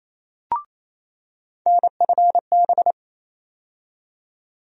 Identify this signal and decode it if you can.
Morse code, heard as a keyed tone, about 28 wpm, 700 Hz. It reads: NFB
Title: USA Call Sign Postfixes - 3 Letters 28wpm